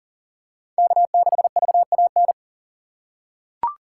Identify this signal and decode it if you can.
Morse code, heard as a keyed tone, about 40 wpm, 700 Hz. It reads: K6VAN
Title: USA Call Signs - 3x Character Spacing - 40wpm